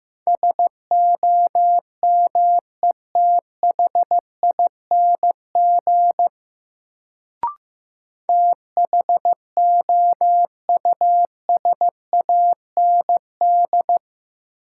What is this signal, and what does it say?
Morse code, heard as a keyed tone, about 16 wpm, 700 Hz. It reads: SOMETHING THOUSAND
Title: Sentences from Top 600 Words - New Word Drill - 15wpm